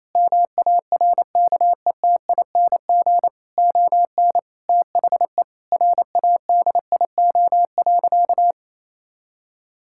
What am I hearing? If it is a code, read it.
Morse code, heard as a keyed tone, about 28 wpm, 700 Hz. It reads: MARKETING ON THE RADIO.
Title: Sentences from Top 1000 Words - Review Part 5 - 28wpm